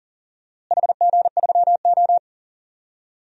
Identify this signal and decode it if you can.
Morse code, heard as a keyed tone, about 40 wpm, 700 Hz. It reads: HG3O
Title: DXCC-100 Call Signs - 4 Characters - Repeated 3x - 40wpm